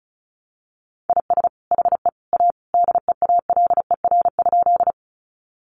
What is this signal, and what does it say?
Morse code, heard as a keyed tone, about 34 wpm, 700 Hz. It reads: IS HE A DEALER?
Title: Sentences from Top 1100 Words 35wpm